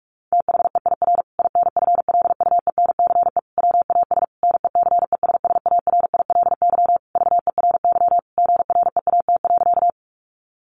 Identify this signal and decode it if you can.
Morse code, heard as a keyed tone, about 44 wpm, 700 Hz. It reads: THEIR INFLUENCE WAS NECESSARILY VERY GREAT.
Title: Sentences from Top 1400 Words 45wpm